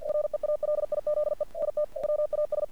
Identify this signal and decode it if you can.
Morse code, heard as a keyed tone, about 47 wpm, 600 Hz. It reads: REALIZE IT WAS
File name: archived / CRSnet